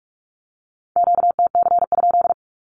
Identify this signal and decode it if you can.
Morse code, heard as a keyed tone, about 46 wpm, 700 Hz. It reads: QTC?
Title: Q-Codes with Spoken Meaning - 45wpm